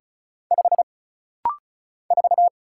Morse code, heard as a keyed tone, about 34 wpm, 700 Hz. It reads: 5 4
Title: Single Letter-Number - Rapid-Fire 35wpm